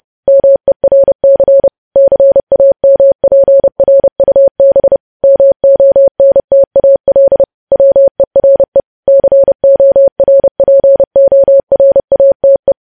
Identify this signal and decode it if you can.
Morse code, heard as a keyed tone, about 30 wpm, 550 Hz. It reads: MERC CAMPRUB MONTAL WERE CORPORATE
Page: news headlines in morse code at 30 wpm